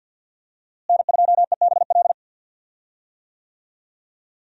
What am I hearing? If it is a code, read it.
Morse code, heard as a keyed tone, about 48 wpm, 700 Hz. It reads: N2EBL
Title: USA Call Signs - 5 Characters - Repeated 3x - 50wpm